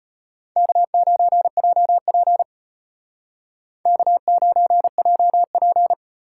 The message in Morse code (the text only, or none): K9JP K9JP